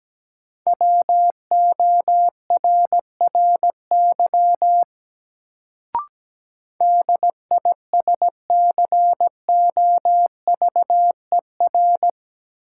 Morse code, heard as a keyed tone, about 17 wpm, 700 Hz. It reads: WORRY DISCOVER
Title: Sentences from Top 1100 Words - New Word Drill - 17wpm